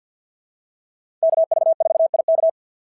Morse code, heard as a keyed tone, about 49 wpm, 690 Hz. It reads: KV4IK
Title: SOTA QSO Practice 50wpm